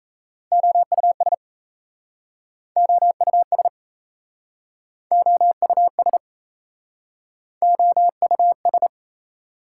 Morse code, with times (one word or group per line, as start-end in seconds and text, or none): 0.5-1.3 OUS
2.8-3.7 OUS
5.1-6.2 OUS
7.6-8.9 OUS